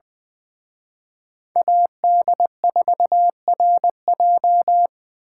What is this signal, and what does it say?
Morse code, heard as a keyed tone, about 24 wpm, 700 Hz. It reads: AD4RJ
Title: Speed Racing - USA 2x2 Call Signs 20wpm